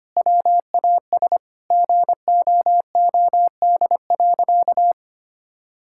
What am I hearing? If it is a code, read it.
Morse code, heard as a keyed tone, about 25 wpm, 700 Hz. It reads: WAS GOOD.
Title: Sentences from Top 1600 Words 25wpm